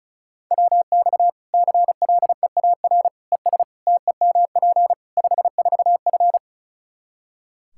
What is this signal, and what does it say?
Morse code, heard as a keyed tone, about 34 wpm, 700 Hz. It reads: WX CLEAR ES TEMP 54F
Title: Practice QSO - Exchange 2 - Info - 35wpm